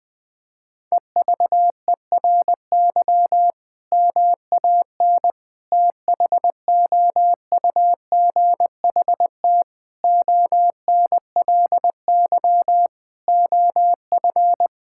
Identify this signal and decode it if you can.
Morse code, heard as a keyed tone, about 20 wpm, 700 Hz. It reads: EVERY MAN THOUGHT ONLY OF
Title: Sentences from Top 1600 Words 20wpm